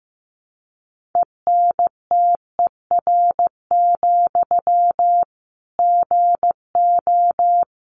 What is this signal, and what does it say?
Morse code, heard as a keyed tone, about 15 wpm, 700 Hz. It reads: ENTER, GO
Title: Sentences from Top 1300 Words 15wpm